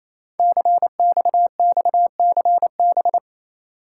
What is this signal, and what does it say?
Morse code, heard as a keyed tone, about 29 wpm, 700 Hz. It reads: CXXCB
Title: Code Groups - 5 Letters - 28wpm